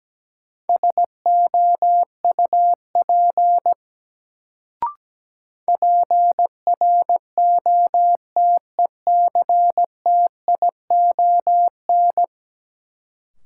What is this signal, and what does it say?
Morse code, heard as a keyed tone, about 17 wpm, 700 Hz. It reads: SOUP PROTECTION
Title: Sentences from Top 1600 Words - New Word Drill Alt - 17wpm